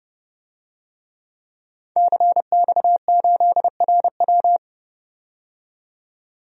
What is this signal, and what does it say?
Morse code, heard as a keyed tone, about 32 wpm, 700 Hz. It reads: CX8RW
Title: DXCC-100 Call Signs - 5 Characters - 30wpm